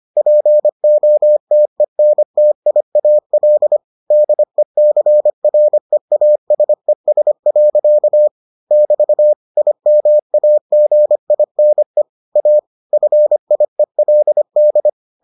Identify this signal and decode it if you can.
Morse code, heard as a keyed tone, about 25 wpm, 600 Hz. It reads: POTENTIAL DECREASES. <BT> IMAGINE A FIELD